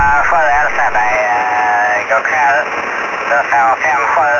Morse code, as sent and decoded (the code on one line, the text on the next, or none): none